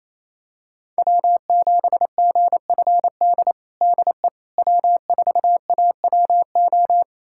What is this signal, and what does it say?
Morse code, heard as a keyed tone, about 28 wpm, 700 Hz. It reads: W7GFD DE W4AWO
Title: USA Callsign DE Callsign - 5 Characters - 28wpm